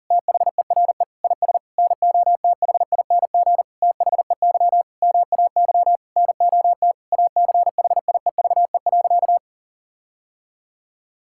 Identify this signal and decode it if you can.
Morse code, heard as a keyed tone, about 37 wpm, 700 Hz. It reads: THERE IS NOTHING THEY MAY NOT ACHIEVE.
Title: Sentences from Top 1000 Words - Review Part 6 - 40wpm